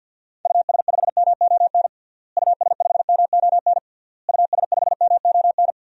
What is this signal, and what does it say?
Morse code, heard as a keyed tone, about 48 wpm, 700 Hz. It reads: US5MON US5MON US5MON
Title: DXCC-100 Call Signs - 6 Characters - Repeated 3x - 50wpm